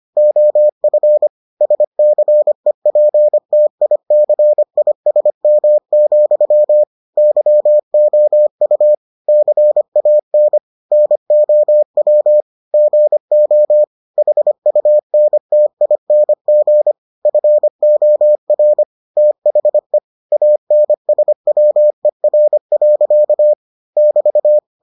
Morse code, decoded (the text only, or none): OF SCEPTICISM, YOU CAN NOW GO HUNTING FOR THE ANSWER. <BT>